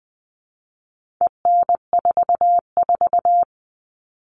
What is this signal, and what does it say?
Morse code, heard as a keyed tone, about 20 wpm, 700 Hz. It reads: EN44
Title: All Cut Numbers - 4-Digit 20wpm